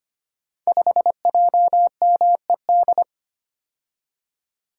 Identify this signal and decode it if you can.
Morse code, heard as a keyed tone, about 25 wpm, 700 Hz. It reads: 5JMED
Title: Code Groups - 5 Letter:Numbers - 25wpm